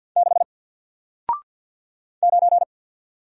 Morse code, heard as a keyed tone, about 50 wpm, 700 Hz. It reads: 6 9
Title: Single Letter-Number Alt 50wpm